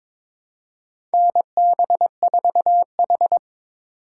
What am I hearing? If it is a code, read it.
Morse code, heard as a keyed tone, about 21 wpm, 700 Hz. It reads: NB4H
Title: USA Call Signs - 4 Characters - 3x Character Spacing - 22wpm